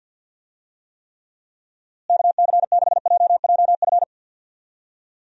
Morse code, heard as a keyed tone, about 50 wpm, 700 Hz. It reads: KC6JJF
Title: USA Call Signs - 6 Characters - Repeated 2x - 50wpm